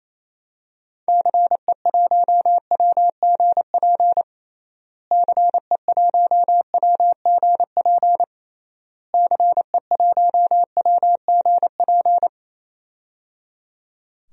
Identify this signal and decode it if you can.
Morse code, heard as a keyed tone, about 28 wpm, 700 Hz. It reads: CE1WGP CE1WGP CE1WGP
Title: DXCC-100 Call Signs - 6 Characters - Repeated 3x - 28wpm